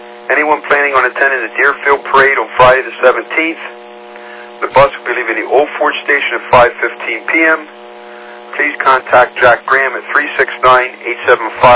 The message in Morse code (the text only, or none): none